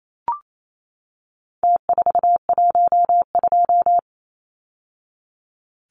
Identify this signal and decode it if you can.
Morse code, heard as a keyed tone, about 28 wpm, 700 Hz. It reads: T412
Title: Numbers - Contest Style - 4-Digit 28wpm